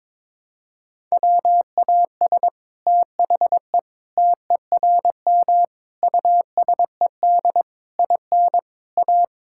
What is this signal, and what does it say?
Morse code, heard as a keyed tone, about 22 wpm, 700 Hz. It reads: WAS THE TERM USED IN A
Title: Sentences from Top 1000 Words - Review Part 6 - 22wpm